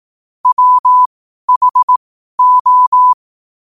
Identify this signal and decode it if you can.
Morse code, heard as a keyed tone, about 18 wpm, spaced as for 12 wpm, 1000 Hz. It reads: WHO
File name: qotdfiles / 12wpm